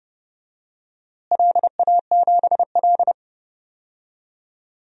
Morse code, heard as a keyed tone, about 32 wpm, 700 Hz. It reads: LA7L
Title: DXCC-100 Call Signs - 4 Characters - Repeated 3x - 30wpm